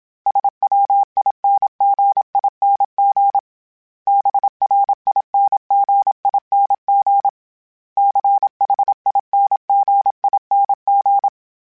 Morse code, heard as a keyed tone, about 27 wpm, 800 Hz. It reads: SWINGING BRINGING CHINGING
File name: Special INGs / Special INGs-mp3s / Dble ings @25